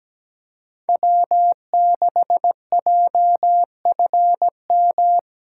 Morse code, heard as a keyed tone, about 20 wpm, 700 Hz. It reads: W6JFM
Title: Speed Racing - USA 1x3 Call Signs 17wpm